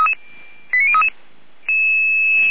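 Five-tone selective call (ZVEI), heard as quick runs of five tones, tones of about 70 ms. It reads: 89033
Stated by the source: ZVEI five-tone selective call 89042, received on SDR.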